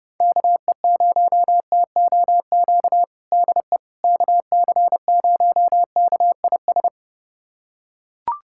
Morse code, heard as a keyed tone, about 32 wpm, 700 Hz. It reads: KE0TOQ DE KC0KIS
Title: USA Callsign DE Callsign - 6 Characters - 30wpm